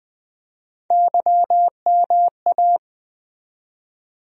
Morse code, heard as a keyed tone, about 20 wpm, 700 Hz. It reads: YMA
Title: USA Call Sign Postfixes - 3 Letters 20wpm